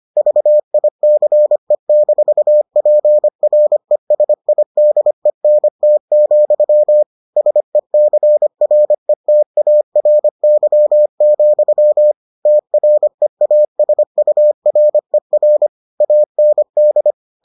Morse code, heard as a keyed tone, about 25 wpm, 600 Hz. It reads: VICE-PRESIDENT, SECRETARY, TREASURER AND